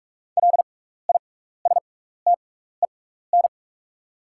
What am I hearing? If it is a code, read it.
Morse code, heard as a keyed tone, about 47 wpm, spaced as for 16 wpm, 690 Hz. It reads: LISTEN
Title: 6x Character Spacing - Top 500 Words - 45wpm